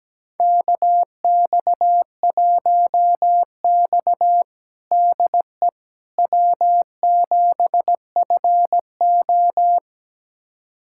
Morse code, heard as a keyed tone, about 17 wpm, 700 Hz. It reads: KX1X DE W7FO
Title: USA Callsign DE Callsign - 4 Characters - 17wpm